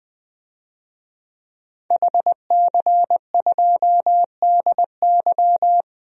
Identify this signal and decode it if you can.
Morse code, heard as a keyed tone, about 20 wpm, 700 Hz. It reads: HC2DY
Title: DXCC-100 Call Signs - 5 Characters - Repeated 3x - 20wpm